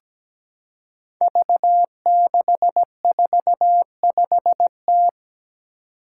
Morse code, heard as a keyed tone, about 17 wpm, 700 Hz. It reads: V645T
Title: Five Challenging Letter-Number Combinations 17wpm